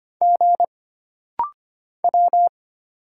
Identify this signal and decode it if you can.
Morse code, heard as a keyed tone, about 25 wpm, 700 Hz. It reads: G W
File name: Single Character (Letter:Number:Punctuation:Prosign) - Rapid-Fire - 25wpm